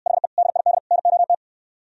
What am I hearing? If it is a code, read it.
Morse code, heard as a keyed tone, about 68 wpm, 700 Hz. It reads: HE DID NOT